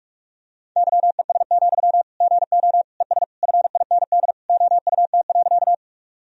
Morse code, heard as a keyed tone, about 42 wpm, 700 Hz. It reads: YES, GO ES FIND OUT.
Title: Sentences from Top 500 Words and Abbreviations 45wpm